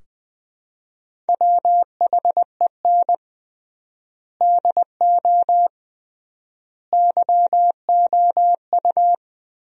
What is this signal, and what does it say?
Morse code, heard as a keyed tone, about 21 wpm, 700 Hz. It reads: WHEN DO YOU
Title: 3x Word-Spacing - Sentences from Top 400 Words - 20wpm